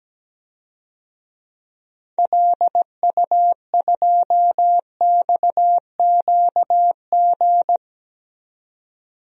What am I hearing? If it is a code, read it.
Morse code, heard as a keyed tone, about 17 wpm, 700 Hz. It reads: LU2XQG
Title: DXCC-100 Call Signs - 6 Characters - 17wpm